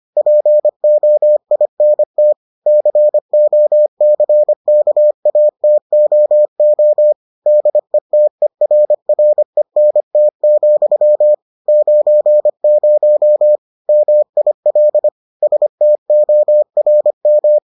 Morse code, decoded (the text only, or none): POINT COCKATOO DETERRENT, 90 MIL STORM